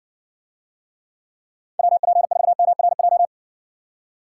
Morse code, heard as a keyed tone, about 57 wpm, 700 Hz. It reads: WP4MRJ